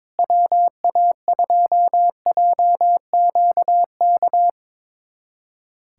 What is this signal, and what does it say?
Morse code, heard as a keyed tone, about 21 wpm, 700 Hz. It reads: WA2JQK
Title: USA Callsign DE Callsign - 6 Characters - 22wpm